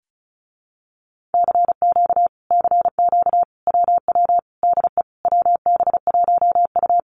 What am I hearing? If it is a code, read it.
Morse code, heard as a keyed tone, about 34 wpm, 700 Hz. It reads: CQ CQ WW DE WB1U
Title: Common QSO Phrases Encore 35wpm